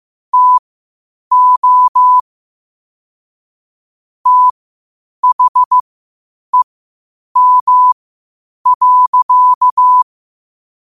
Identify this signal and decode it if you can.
Morse code, heard as a keyed tone, about 15 wpm, spaced as for 8 wpm, 1000 Hz. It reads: TO THEM.
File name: qotdfiles / 8wpm